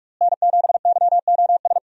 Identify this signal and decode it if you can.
Morse code, heard as a keyed tone, about 43 wpm, 700 Hz. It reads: N7YOS
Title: USA Call Signs - 5 Characters - Repeated 2x - 45wpm